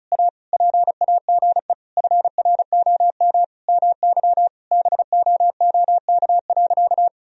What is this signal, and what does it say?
Morse code, heard as a keyed tone, about 34 wpm, 700 Hz. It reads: A PAGE FROM MY BOOK.
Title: Sentences from Top 1000 Words - Review Part 1 - 35wpm